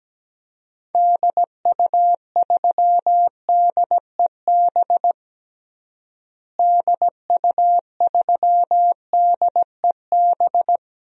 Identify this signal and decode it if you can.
Morse code, heard as a keyed tone, about 17 wpm, 700 Hz. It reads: DU3DEB DU3DEB
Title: DXCC-100 Call Signs - 6 Characters - Repeated 3x - 17wpm